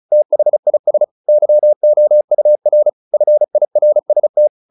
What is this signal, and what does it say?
Morse code, heard as a keyed tone, about 34 wpm, 600 Hz. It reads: THIS YOUR FIRST
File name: the old man and the sea / the old man and the sea 1 35wpm